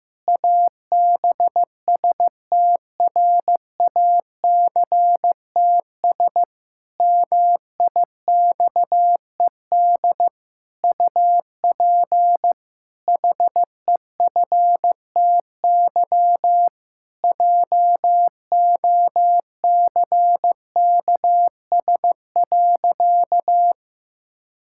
Morse code, heard as a keyed tone, about 15 wpm, 700 Hz. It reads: ABSTRACTS MIXED UP HEFTY JOCKS.
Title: Pangram Challenge v2 - 15wpm